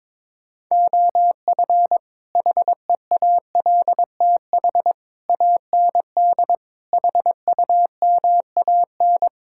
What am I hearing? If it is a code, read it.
Morse code, heard as a keyed tone, about 22 wpm, 700 Hz. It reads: OF HEALTH AND HUMAN
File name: practice / 22wpm / Build A Phrase - 5-Word Phrases - 22wpm